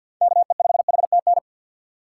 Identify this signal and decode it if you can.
Morse code, heard as a keyed tone, about 48 wpm, 700 Hz. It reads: KE5HTN
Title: USA Call Signs - 6 Characters - 6x Character Spacing - 50wpm